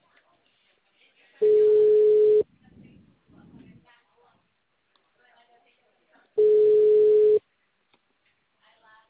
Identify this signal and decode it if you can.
Ringback tone (Europe, 425 Hz), with speech.